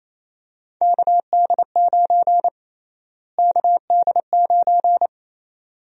Morse code, heard as a keyed tone, about 27 wpm, 700 Hz. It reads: KD9 KD9
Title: USA Call Sign Prefixes - Repeated 2x - 28wpm